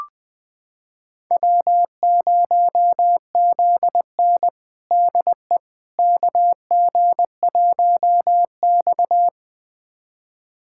Morse code, heard as a keyed tone, about 21 wpm, 700 Hz. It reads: W0ZN DE KG1X
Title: USA Callsign DE Callsign - 4 Characters - 20wpm